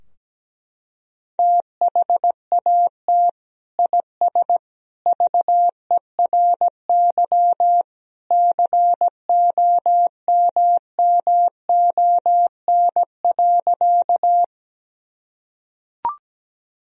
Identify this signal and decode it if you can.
Morse code, heard as a keyed tone, about 17 wpm, 700 Hz. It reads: THAT IS VERY COMMON.
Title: Sentences from Top 1000 Words - Review Part 3 - 17wpm